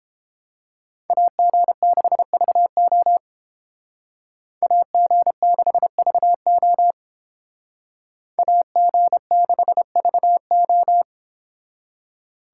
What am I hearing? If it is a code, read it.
Morse code, heard as a keyed tone, about 29 wpm, 700 Hz. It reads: AG6VO AG6VO AG6VO